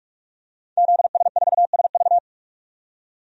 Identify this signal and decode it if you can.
Morse code, heard as a keyed tone, about 41 wpm, 700 Hz. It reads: ZS4SV
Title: DXCC-100 Call Signs - 5 Characters - 45wpm